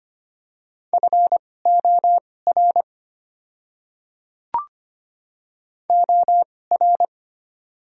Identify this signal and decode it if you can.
Morse code, heard as a keyed tone, about 25 wpm, spaced as for 18 wpm, 700 Hz. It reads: FOR OR
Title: 2x Character Spacing - Top 500 Words - 25wpm